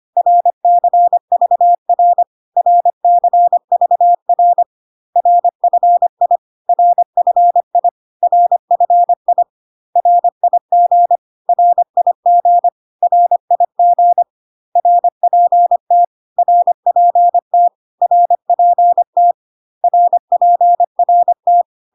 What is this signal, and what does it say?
Morse code, heard as a keyed tone, about 25 wpm, 700 Hz. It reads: RCVR RCVR RFI RFI RFI RIG RIG RIG RPT RPT RPT RPRT
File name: cw / CW common abbreviations alphabetical repeated 3 times 25 25wpm ik1hge